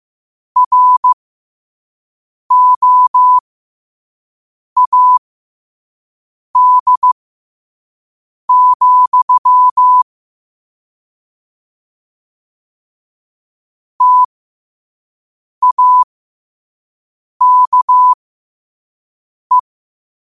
Morse code, heard as a keyed tone, about 15 wpm, spaced as for 5 wpm, 1000 Hz. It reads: ROAD, TAKE
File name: qotdfiles / 5wpm